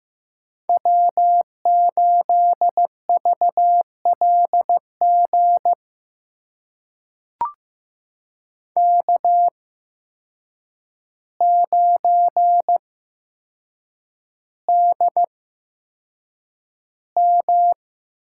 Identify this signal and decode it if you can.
Morse code, heard as a keyed tone, about 15 wpm, spaced as for 7 wpm, 700 Hz. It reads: W8VLG K9DM
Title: USA Call Signs - 5 Characters - 8x Character Spacing - 15wpm